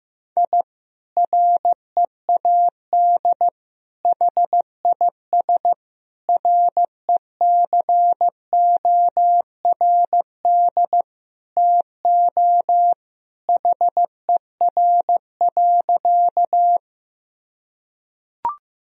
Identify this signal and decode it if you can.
Morse code, heard as a keyed tone, about 15 wpm, 700 Hz. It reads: I READ HIS RECORD TO HER.
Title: Sentences from Top 1000 Words - Review Part 3 - 15wpm